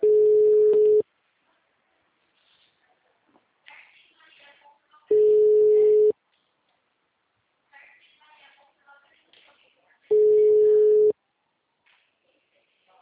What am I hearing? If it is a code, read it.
Ringback tone (Europe, 425 Hz), with speech.